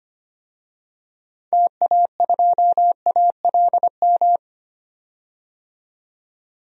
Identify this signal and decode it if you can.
Morse code, heard as a keyed tone, about 25 wpm, 700 Hz. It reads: TA2ALM
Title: DXCC-100 Call Signs - 6 Characters - 25wpm